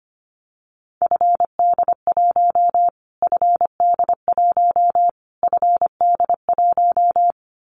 Morse code, heard as a keyed tone, about 25 wpm, 700 Hz. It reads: FD1 FD1 FD1
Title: Callsign Prefixes for Top 25 DXCC - Repeated 3x - 25wpm